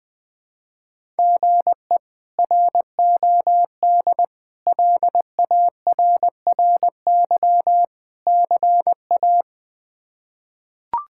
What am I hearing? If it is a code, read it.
Morse code, heard as a keyed tone, about 21 wpm, 700 Hz. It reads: GE ROD LARRY CA
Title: K1USN SST QSO Practice - 20wpm